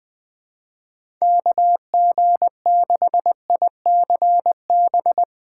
Morse code, heard as a keyed tone, about 21 wpm, 700 Hz. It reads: KG6ICB
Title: USA Call Signs - 2x Character Spacing - 20wpm